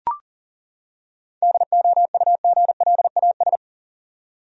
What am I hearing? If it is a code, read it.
Morse code, heard as a keyed tone, about 38 wpm, 700 Hz. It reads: DOUGLAS
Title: USA Common Names - 40wpm